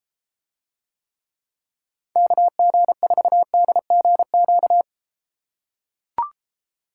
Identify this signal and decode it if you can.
Morse code, heard as a keyed tone, about 32 wpm, 700 Hz. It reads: KG4DGQ